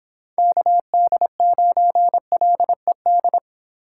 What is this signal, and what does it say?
Morse code, heard as a keyed tone, about 26 wpm, 700 Hz. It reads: KD9LED
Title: Speed Racing - USA Call Signs 22wpm